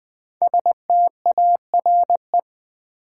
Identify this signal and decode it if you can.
Morse code, heard as a keyed tone, about 20 wpm, 700 Hz. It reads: STARE
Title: 5 Letter Words - First 3 Letters Given - 20wpm